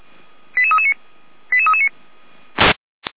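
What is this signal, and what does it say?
Five-tone selective call (ZVEI), heard as quick runs of five tones, tones of about 70 ms. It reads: 88339 88339
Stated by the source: ZVEI five-tone selective call 88341 received on SDR, from feuerwehr-orlamuende fire brigade alerting.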